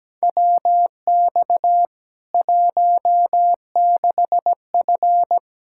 Morse code, heard as a keyed tone, about 17 wpm, 700 Hz. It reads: WX 16F
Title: K1USN SST QSO Practice 17wpm